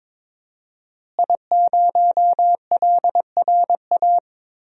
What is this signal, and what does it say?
Morse code, heard as a keyed tone, about 22 wpm, 700 Hz. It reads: I0LRA
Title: DXCC-100 Call Signs - 5 Characters - Repeated 3x - 22wpm